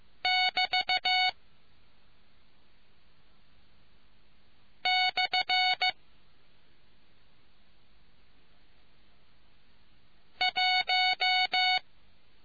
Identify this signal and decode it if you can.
Morse code, heard as a keyed tone, about 15 wpm, 740 Hz. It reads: <BT> / 1